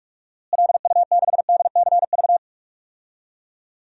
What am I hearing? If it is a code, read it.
Morse code, heard as a keyed tone, about 46 wpm, 700 Hz. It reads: LU6DCV
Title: DXCC-100 Call Signs - 6 Characters - Repeated 2x - 45wpm